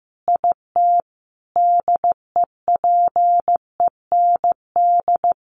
Morse code, heard as a keyed tone, about 15 wpm, 700 Hz. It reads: IT DEPEND
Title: Sentences from Top 1300 Words 15wpm